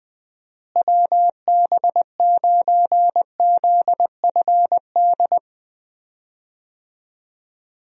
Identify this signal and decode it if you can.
Morse code, heard as a keyed tone, about 20 wpm, 700 Hz. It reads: WB9ZFD
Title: USA Call Signs - 6 Characters - Repeated 2x - 20wpm